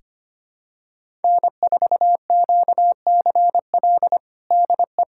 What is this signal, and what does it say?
Morse code, heard as a keyed tone, about 25 wpm, 700 Hz. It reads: N4QCL DE
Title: USA Callsign DE Callsign - 5 Characters - 25wpm